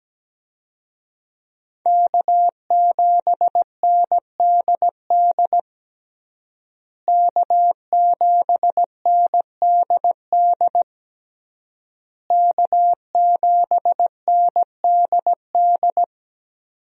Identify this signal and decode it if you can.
Morse code, heard as a keyed tone, about 17 wpm, 700 Hz. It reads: K7NDD K7NDD K7NDD
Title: USA Call Signs - 5 Characters - Repeated 3x - 17wpm